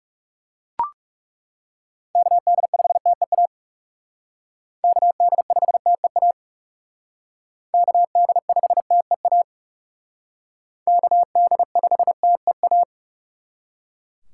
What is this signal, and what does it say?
Morse code, heard as a keyed tone, about 40 wpm, 700 Hz. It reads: KD5TEA KD5TEA KD5TEA KD5TEA